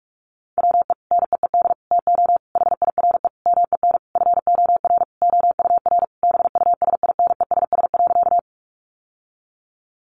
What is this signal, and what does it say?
Morse code, heard as a keyed tone, about 44 wpm, 700 Hz. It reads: WE NEED TO HIRE MEN FOR OUR BUSINESS.
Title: Sentences from Top 1400 Words 45wpm